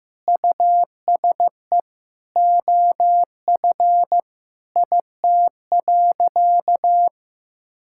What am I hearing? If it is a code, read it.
Morse code, heard as a keyed tone, about 15 wpm, 700 Hz. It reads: USE OF IT.
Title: Sentences from Top 1000 Words - Review Part 6 - 15wpm